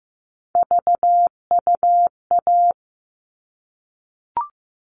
Morse code, heard as a keyed tone, about 15 wpm, 700 Hz. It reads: VUA
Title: All Cut Numbers - 3-Digit 15wpm